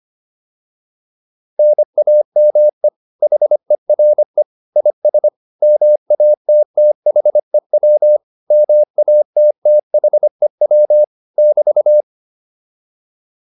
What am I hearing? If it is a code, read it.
Morse code, heard as a keyed tone, about 25 wpm, 600 Hz. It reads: NAME HERE IS MATTHEW MATTHEW <BT>
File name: QSO308 25